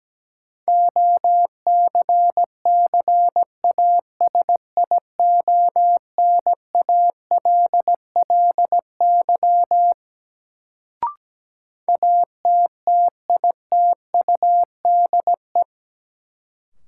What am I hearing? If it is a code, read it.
Morse code, heard as a keyed tone, about 17 wpm, 700 Hz. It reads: OCCASIONALLY ATTITUDE
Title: Sentences from Top 1300 Words - New Word Drill - 17wpm